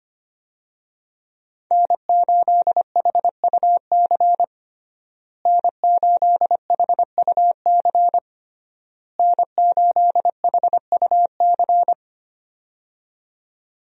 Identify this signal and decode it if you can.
Morse code, heard as a keyed tone, about 25 wpm, 700 Hz. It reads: N8HUC N8HUC N8HUC
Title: USA Call Signs - 5 Characters - Repeated 3x - 25wpm